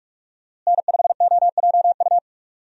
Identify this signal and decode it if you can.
Morse code, heard as a keyed tone, about 46 wpm, 700 Hz. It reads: N5OJU
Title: USA Call Signs - 5 Characters - Repeated 2x - 45wpm